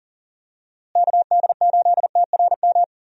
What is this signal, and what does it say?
Morse code, heard as a keyed tone, about 40 wpm, 700 Hz. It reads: KD8TRM
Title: USA Call Signs - 6 Characters - Repeated 2x - 40wpm